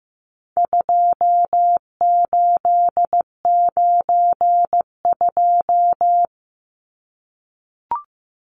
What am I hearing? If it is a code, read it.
Morse code, heard as a keyed tone, about 15 wpm, 700 Hz. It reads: 2892
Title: Numbers - 4-Digit 15wpm